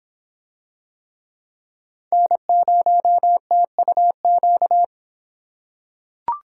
Morse code, heard as a keyed tone, about 24 wpm, 700 Hz. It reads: N0TUQ